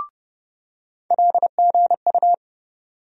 Morse code, heard as a keyed tone, about 30 wpm, 700 Hz. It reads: LGU
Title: USA Call Sign Postfixes - 3 Letters 30wpm